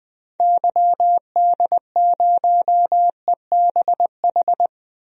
Morse code, heard as a keyed tone, about 20 wpm, 700 Hz. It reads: YD0EBH
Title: DXCC-100 Call Signs - 6 Characters - 20wpm